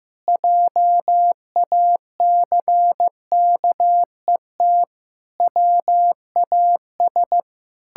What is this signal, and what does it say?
Morse code, heard as a keyed tone, about 16 wpm, 700 Hz. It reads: JACKET WAS
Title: Pangram Challenge v2 - 15wpm